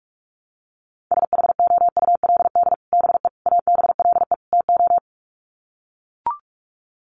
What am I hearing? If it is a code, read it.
Morse code, heard as a keyed tone, about 44 wpm, 700 Hz. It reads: SHOULD BE ABLE TO
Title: Sets of 4 Words 45wpm